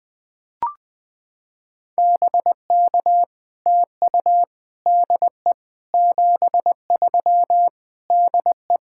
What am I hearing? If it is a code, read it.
Morse code, heard as a keyed tone, about 21 wpm, 700 Hz. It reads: BK TU DE 73 DE
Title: POTA QSO Practice - 20wpm